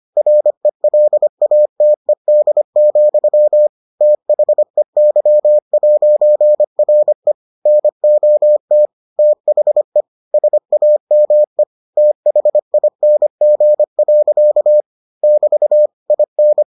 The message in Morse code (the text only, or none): RELATED, THEY'RE NOT THE SAME THING. <BT> IN